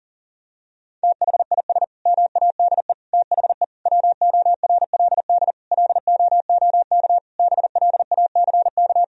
Morse code, heard as a keyed tone, about 40 wpm, 700 Hz. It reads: THIS MADE THE WORLD LOOK BLACK
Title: Sentences from Top 1000 Words - Review Part 2 - 40wpm